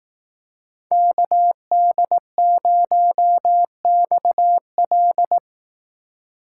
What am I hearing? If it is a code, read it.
Morse code, heard as a keyed tone, about 18 wpm, 700 Hz. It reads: KD0XL
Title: Speed Racing - USA 2x2 Call Signs 15wpm